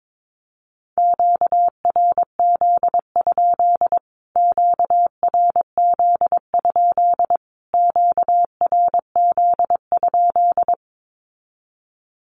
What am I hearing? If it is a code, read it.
Morse code, heard as a keyed tone, about 22 wpm, 700 Hz. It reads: QRZ? QRZ? QRZ?
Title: Q-Codes Repeated 3x 22wpm